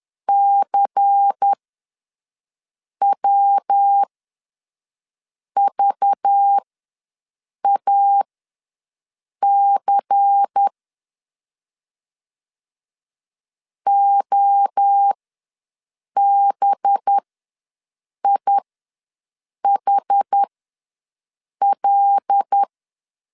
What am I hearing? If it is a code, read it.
Morse code, heard as a keyed tone, about 10 wpm, spaced as for 5 wpm, 800 Hz. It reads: CWVAC OBIHL
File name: LEZIONI / 5 LBVC / 5 LBVC ws 10wpm 3733